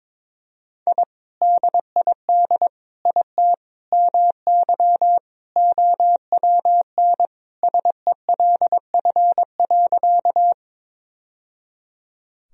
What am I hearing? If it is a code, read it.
Morse code, heard as a keyed tone, about 21 wpm, 700 Hz. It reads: I DID IT MY OWN SELF.
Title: Sentences from Top 1000 Words - Review Part 1 - 22wpm